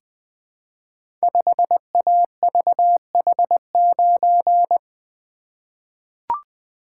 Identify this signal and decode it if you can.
Morse code, heard as a keyed tone, about 20 wpm, 700 Hz. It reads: 5AVH9
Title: Code Groups - 5 Letter:Numbers - 20wpm